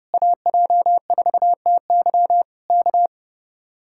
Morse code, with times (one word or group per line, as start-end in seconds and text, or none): 0.1-2.4 AJ4TY
2.7-3.1 K